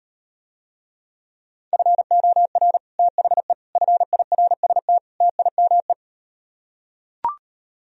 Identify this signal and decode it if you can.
Morse code, heard as a keyed tone, about 38 wpm, 700 Hz. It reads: FOR THE FIRST TIME